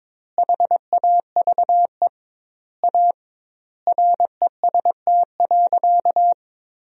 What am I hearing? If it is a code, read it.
Morse code, heard as a keyed tone, about 22 wpm, 700 Hz. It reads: HAVE A REST.
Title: 2x Word-Spacing - Sentences from Top 500 Words - 22wpm